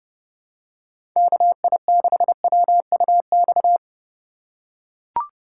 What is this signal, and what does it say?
Morse code, heard as a keyed tone, about 32 wpm, 700 Hz. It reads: KI6WUX
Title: USA Call Signs - 6 Characters - 4x Character Spacing - 30wpm